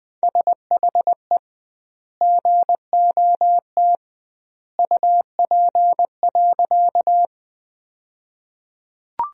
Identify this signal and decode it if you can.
Morse code, heard as a keyed tone, about 21 wpm, 700 Hz. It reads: SHE GOT UP.
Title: 2x Word-Spacing - Sentences from Top 300 Words - 20wpm